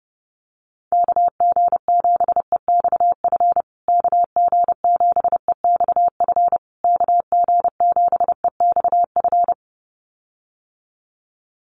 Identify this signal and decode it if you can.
Morse code, heard as a keyed tone, about 30 wpm, 700 Hz. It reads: KG7EXF KG7EXF KG7EXF
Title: USA Call Signs Repeated 3x 30wpm